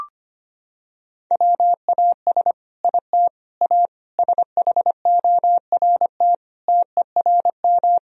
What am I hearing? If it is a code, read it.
Morse code, heard as a keyed tone, about 25 wpm, 700 Hz. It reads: WAS IT A SHORT TERM
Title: Sentences from Top 1000 Words - Review Part 4 - 25wpm